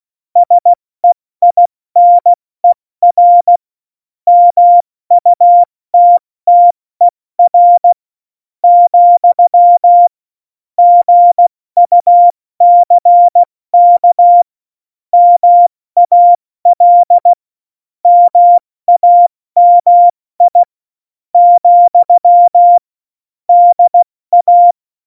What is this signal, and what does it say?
Morse code, heard as a keyed tone, about 16 wpm, spaced as for 13 wpm, 700 Hz. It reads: SEINER MUTTER, GUCK MAL MAMI, DA
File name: Morsekurs / Witze / Witz 21. August 2018, 70 F80